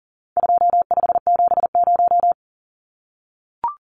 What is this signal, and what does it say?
Morse code, heard as a keyed tone, about 40 wpm, 700 Hz. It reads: 2570
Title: Numbers - 4-Digit 40wpm